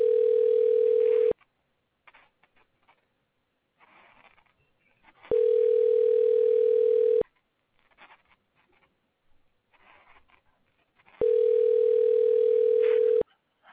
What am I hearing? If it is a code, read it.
Ringback tone (North America), 440 + 480 Hz.